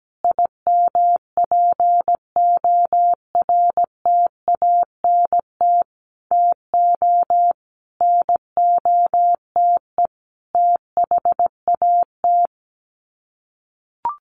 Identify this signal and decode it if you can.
Morse code, heard as a keyed tone, about 17 wpm, 700 Hz. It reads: IMPORTANT TO NOTE THAT
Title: Sets of 5 Words 17wpm